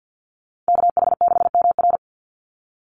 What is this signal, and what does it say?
Morse code, heard as a keyed tone, about 50 wpm, 700 Hz. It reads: KH6MR
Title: USA Call Signs 50wpm